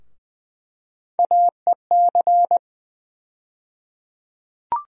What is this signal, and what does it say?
Morse code, heard as a keyed tone, about 20 wpm, 700 Hz. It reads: AEC
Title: USA Call Sign Postfixes - 3 Letters 20wpm